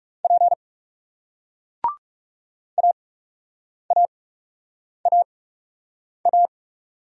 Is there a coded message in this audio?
Yes: Morse code from 0.2 to 6.5 s, heard as a keyed tone, about 40 wpm, spaced as for 8 wpm, 700 Hz. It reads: P AAAA